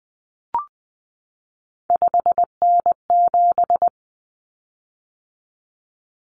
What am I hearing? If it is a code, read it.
Morse code, heard as a keyed tone, about 20 wpm, 700 Hz. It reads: 5N7
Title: Common Cut Numbers - 3-Digit 20wpm